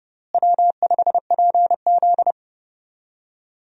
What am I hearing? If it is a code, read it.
Morse code, heard as a keyed tone, about 30 wpm, 700 Hz. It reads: W5PZ
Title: USA Call Signs - 4 Characters - Repeated 3x - 30wpm